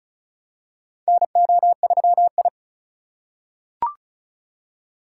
Morse code, heard as a keyed tone, about 36 wpm, 700 Hz. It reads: NO3I
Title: USA Call Signs - 4 Characters - 2x Character Spacing - 35wpm